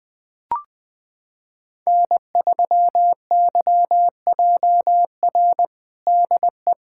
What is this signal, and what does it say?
Morse code, heard as a keyed tone, about 20 wpm, 700 Hz. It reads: N3YJR DE
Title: USA Callsign DE Callsign - 5 Characters - 20wpm